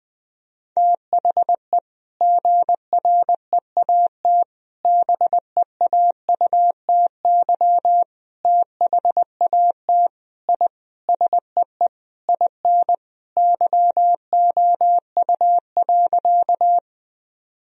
Morse code, heard as a keyed tone, about 20 wpm, 700 Hz. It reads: THE GREAT BEAUTY THAT I SEE IN YOU.
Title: Sentences from Top 1000 Words - Review Part 2 - 20wpm